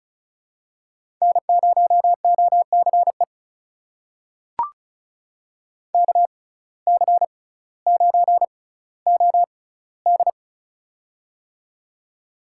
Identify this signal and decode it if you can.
Morse code, heard as a keyed tone, about 34 wpm, spaced as for 18 wpm, 700 Hz. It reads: N0OCE KC9OD
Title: USA Call Signs - 5 Characters - 6x Character Spacing - 35wpm